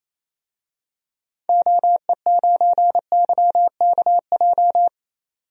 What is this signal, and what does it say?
Morse code, heard as a keyed tone, about 28 wpm, 700 Hz. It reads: OE9YKJ
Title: DXCC-100 Call Signs - 6 Characters - 28wpm